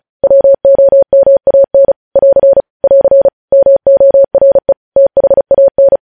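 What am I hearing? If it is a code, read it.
Morse code, heard as a keyed tone, about 36 wpm, 550 Hz. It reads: WOMAN <AR> <AR> MORE THAN
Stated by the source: Morse code news headlines sent at 35 wpm.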